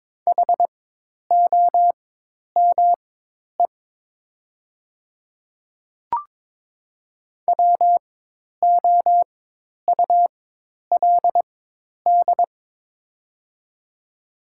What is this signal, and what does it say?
Morse code, heard as a keyed tone, about 22 wpm, spaced as for 10 wpm, 700 Hz. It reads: HOME WOULD
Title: 4x Character Spacing - Top 300 Words - 22wpm